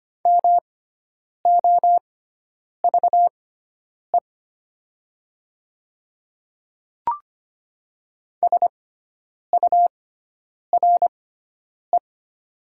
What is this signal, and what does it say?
Morse code, heard as a keyed tone, about 25 wpm, spaced as for 9 wpm, 700 Hz. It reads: MOVE SURE
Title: 6x Character Spacing - Top 500 Words - 25wpm